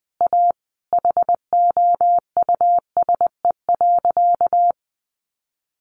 Morse code, heard as a keyed tone, about 20 wpm, 700 Hz. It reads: A HOUSE.
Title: Sentences from Top 600 Words 20wpm